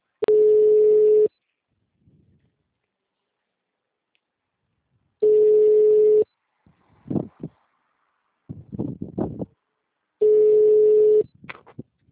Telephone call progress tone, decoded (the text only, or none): ringback tone